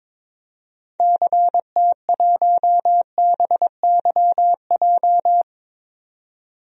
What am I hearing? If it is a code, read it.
Morse code, heard as a keyed tone, about 22 wpm, 700 Hz. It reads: CT1BYJ
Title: DXCC-100 Call Signs - 6 Characters - 22wpm